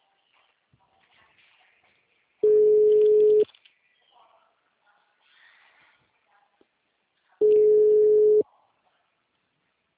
Ringback tone (Europe, 425 Hz).